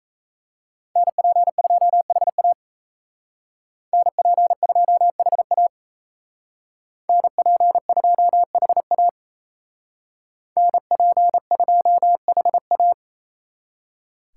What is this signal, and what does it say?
Morse code, heard as a keyed tone, about 35 wpm, 700 Hz. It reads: NP2HA NP2HA NP2HA NP2HA
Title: Speed Racing - USA 2x2 Call Signs 28wpm